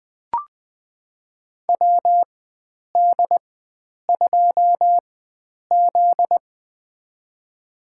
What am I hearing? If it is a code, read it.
Morse code, heard as a keyed tone, about 20 wpm, spaced as for 9 wpm, 700 Hz. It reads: WD2Z
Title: USA Call Signs - 4x Character Spacing - 20wpm